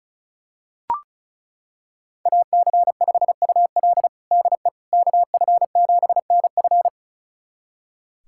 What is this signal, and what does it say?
Morse code, heard as a keyed tone, about 36 wpm, 700 Hz. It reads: AC5UL DE KF7NF
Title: USA Callsign DE Callsign 35wpm